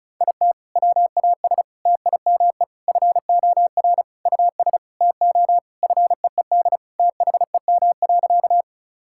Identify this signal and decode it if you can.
Morse code, heard as a keyed tone, about 36 wpm, 700 Hz. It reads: IT WAS TIME FOR US TO FEED THEM.
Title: Sentences from Top 1600 Words 35wpm